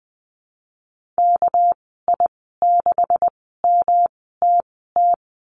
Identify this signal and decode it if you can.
Morse code, heard as a keyed tone, about 20 wpm, spaced as for 14 wpm, 700 Hz. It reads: KI6MTT
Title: 200 Lesson 32g - Letter k - Call Signs 20wpm